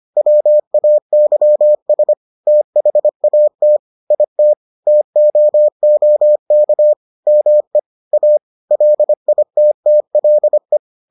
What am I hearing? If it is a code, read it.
Morse code, heard as a keyed tone, about 25 wpm, 600 Hz. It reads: WAYS THAT IT TOOK ME A LITTLE